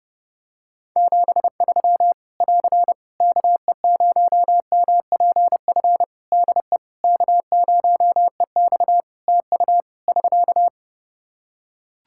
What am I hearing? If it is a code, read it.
Morse code, heard as a keyed tone, about 30 wpm, 700 Hz. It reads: 73 <AR> KE0MPF DE K0EX TU <SK>
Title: Practice QSO - Exchange 4 - Ending - 30wpm